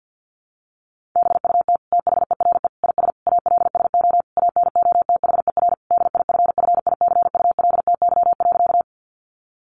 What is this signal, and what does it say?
Morse code, heard as a keyed tone, about 50 wpm, 700 Hz. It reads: BUT THERE IS ALSO ANOTHER DIFFICULTY.
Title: Sentences from Top 1500 Words 50wpm